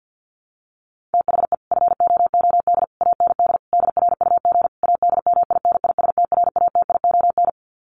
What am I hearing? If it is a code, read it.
Morse code, heard as a keyed tone, about 50 wpm, 700 Hz. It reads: THE FOOD AND DRUG ADMINISTRATION
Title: Sets of 5 Words 50wpm